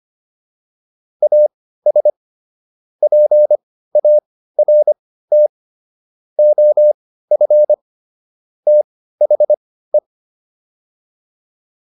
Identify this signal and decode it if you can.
Morse code, heard as a keyed tone, about 25 wpm, spaced as for 15 wpm, 600 Hz. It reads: AS PART OF THE